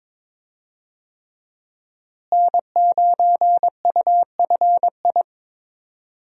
Morse code, heard as a keyed tone, about 22 wpm, 700 Hz. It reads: N9UFI
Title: USA Call Signs - 5 Characters - Repeated 3x - 22wpm